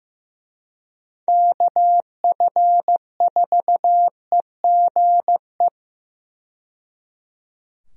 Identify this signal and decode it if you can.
Morse code, heard as a keyed tone, about 15 wpm, 700 Hz. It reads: KF4EGE
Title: USA Call Signs - 6 Characters - 15wpm